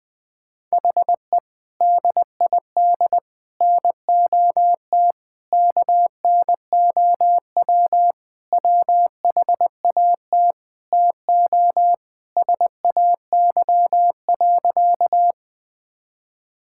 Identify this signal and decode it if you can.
Morse code, heard as a keyed tone, about 21 wpm, 700 Hz. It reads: HE DID NOT KNOW WHAT TO SAY.
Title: Sentences with 3-Word Phrases - 20wpm